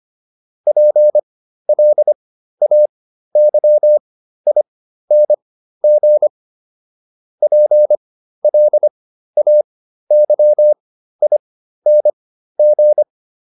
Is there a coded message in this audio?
Yes: Morse code from 0.7 to 13.0 s, heard as a keyed tone, about 25 wpm, spaced as for 13 wpm, 600 Hz. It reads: PLAYING PLAYING